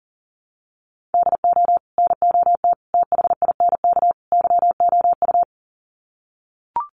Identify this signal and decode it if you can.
Morse code, heard as a keyed tone, about 40 wpm, 700 Hz. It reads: DO NOT THINK YOU
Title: Sets of 4 Words 40wpm